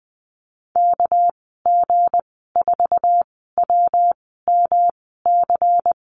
Morse code, heard as a keyed tone, about 20 wpm, spaced as for 15 wpm, 700 Hz. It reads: KG4WMC
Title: 219 Lesson 35g - Number 0 - Call Signs 20wpm